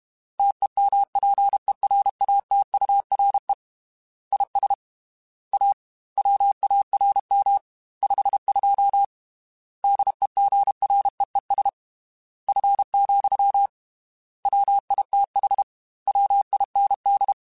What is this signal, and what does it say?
Morse code, heard as a keyed tone, about 32 wpm, 800 Hz. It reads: TEMPERATURE IS A WARM 52 DEGREES F, WITH WIND
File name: Weather Sets / Weather Sets-mp3s / Weather 4 10 @30